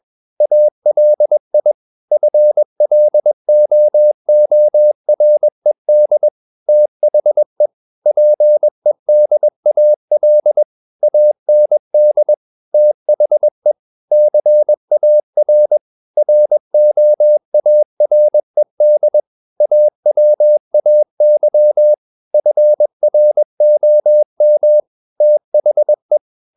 Morse code, heard as a keyed tone, about 21 wpm, 600 Hz. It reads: ALI FLOORED THE PEDAL AND THE CAR ROARED AWAY FROM THE
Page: Romanen Never af Ken Follet på engelsk.